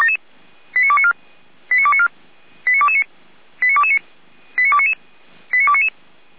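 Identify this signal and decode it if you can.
Five-tone selective call (ZVEI), heard as quick runs of five tones, tones of about 70 ms. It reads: 89294 89294 89229 89229 89299 89299
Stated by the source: Five-tone ZVEI selective call 89276, received on SDR, from the feuerwehr-orlamuende.org fire brigade audio server.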